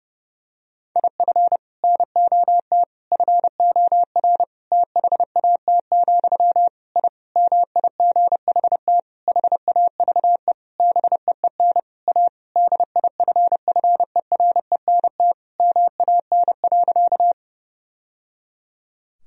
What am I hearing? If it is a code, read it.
Morse code, heard as a keyed tone, about 28 wpm, 700 Hz. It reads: IF NOT FOR THAT, I MIGHT HAVE BEEN A DIFFERENT MAN.
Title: Sentences from Top 1000 Words - Review Part 3 - 30wpm